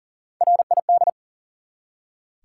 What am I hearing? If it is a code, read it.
Morse code, heard as a keyed tone, about 40 wpm, 700 Hz. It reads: RID
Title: Three Letter Words - 40wpm